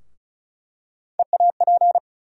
Morse code, heard as a keyed tone, about 36 wpm, 700 Hz. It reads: EAP
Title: USA Call Sign Postfixes - 3 Letters 35wpm